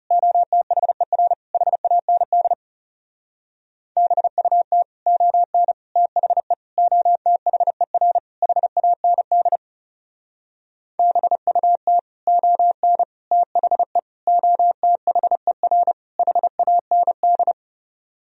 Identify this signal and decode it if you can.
Morse code, heard as a keyed tone, about 38 wpm, 700 Hz. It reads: OTHER HAND BUT ON THE OTHER HAND BUT ON THE OTHER HAND